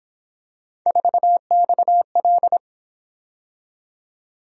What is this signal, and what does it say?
Morse code, heard as a keyed tone, about 28 wpm, 700 Hz. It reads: 4XL